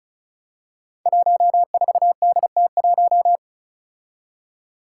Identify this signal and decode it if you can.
Morse code, heard as a keyed tone, about 36 wpm, 700 Hz. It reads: 14DT1
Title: Five Challenging Letter-Number Combinations 35wpm